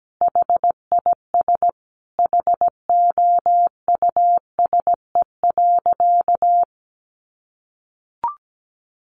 Morse code, heard as a keyed tone, about 17 wpm, 700 Hz. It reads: HIS HOUSE.
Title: Sentences from Top 1000 Words 17wpm